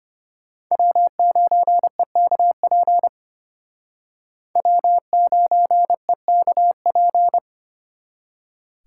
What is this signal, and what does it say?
Morse code, heard as a keyed tone, about 32 wpm, 700 Hz. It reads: W9EKP W9EKP